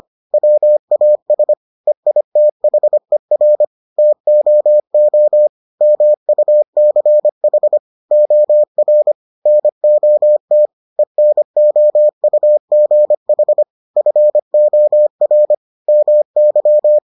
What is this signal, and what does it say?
Morse code, heard as a keyed tone, about 25 wpm, 600 Hz. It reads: WAS EITHER TOO MUCH OR NOT ENOUGH FOR MY